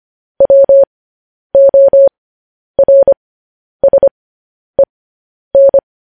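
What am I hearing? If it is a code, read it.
Morse code, heard as a keyed tone, about 25 wpm, spaced as for 10 wpm, 550 Hz. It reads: WORSEN
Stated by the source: Morse code transmission of news headlines.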